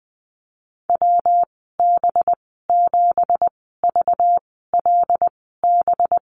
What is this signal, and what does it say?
Morse code, heard as a keyed tone, about 20 wpm, spaced as for 15 wpm, 700 Hz. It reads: WB7VLB
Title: 193 Lesson 31g - Letter v - Call Signs 20wpm